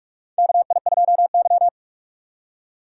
Morse code, heard as a keyed tone, about 43 wpm, 700 Hz. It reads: KI2Y
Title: USA Call Signs - 4 Characters - 6x Character Spacing - 45wpm